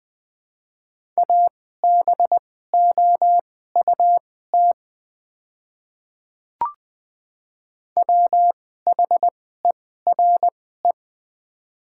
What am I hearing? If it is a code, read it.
Morse code, heard as a keyed tone, about 20 wpm, spaced as for 15 wpm, 700 Hz. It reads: ABOUT WHERE
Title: 2x Character Spacing - Top 300 Words - 20wpm